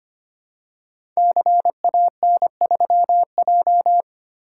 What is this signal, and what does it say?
Morse code, heard as a keyed tone, about 25 wpm, 700 Hz. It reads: CAN3J
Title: Code Groups - 5 Letter:Numbers - 25wpm